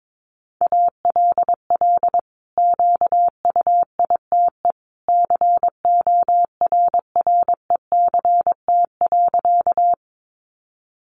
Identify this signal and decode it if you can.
Morse code, heard as a keyed tone, about 22 wpm, 700 Hz. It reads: ALL QUITE CORRECT.
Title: Sentences from Top 700 Words 22wpm